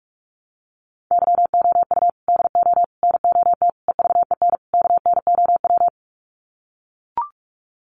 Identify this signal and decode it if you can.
Morse code, heard as a keyed tone, about 44 wpm, 690 Hz. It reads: YOU DO NOT EVEN KNOW
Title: Sets of 5 Words Encore2 45wpm